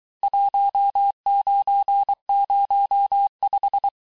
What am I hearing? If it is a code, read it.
Morse code, heard as a keyed tone, about 23 wpm, 800 Hz. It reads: 1905
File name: texte-clair-vitesse-17mpm-puis-22mpm